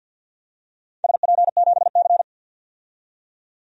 Morse code, heard as a keyed tone, about 47 wpm, 700 Hz. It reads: SP7C
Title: DXCC-100 Call Signs - 4 Characters - 50wpm